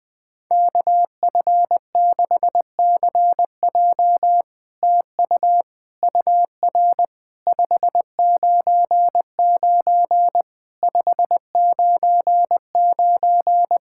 Morse code, heard as a keyed tone, about 20 wpm, 700 Hz. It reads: KF6CJ TU UR 599 599
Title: SOTA QSO Practice 20wpm